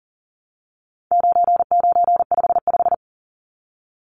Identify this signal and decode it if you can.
Morse code, heard as a keyed tone, about 40 wpm, 700 Hz. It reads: 9955
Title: Numbers - 4-Digit 40wpm